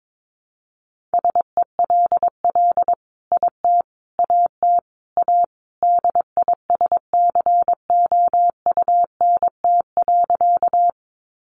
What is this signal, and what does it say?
Morse code, heard as a keyed tone, about 22 wpm, 700 Hz. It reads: SELL IT AT A DISCOUNT.
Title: Sentences from Top 1200 Words 22wpm